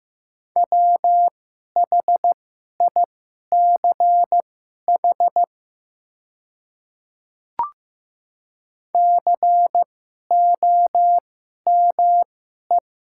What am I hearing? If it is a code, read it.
Morse code, heard as a keyed tone, about 15 wpm, spaced as for 11 wpm, 700 Hz. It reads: WHICH COME
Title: 2x Character Spacing - Top 100 Words - 15wpm